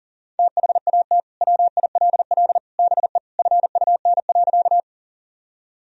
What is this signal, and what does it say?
Morse code, heard as a keyed tone, about 40 wpm, 700 Hz. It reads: THAT WILL BE FUN.
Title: Sentences from Top 1000 Words - Review Part 4 - 40wpm